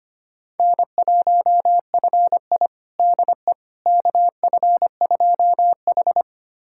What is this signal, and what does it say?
Morse code, heard as a keyed tone, about 25 wpm, 700 Hz. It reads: N1FI DE KF2H
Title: USA Callsign DE Callsign - 4 Characters - 25wpm